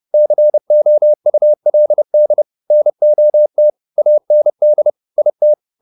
Morse code, heard as a keyed tone, about 30 wpm, 600 Hz. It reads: COULD NOT AND IT